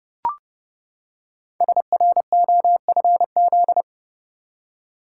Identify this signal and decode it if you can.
Morse code, heard as a keyed tone, about 30 wpm, 700 Hz. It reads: SROFZ
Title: Code Groups - 5 Letters - 30wpm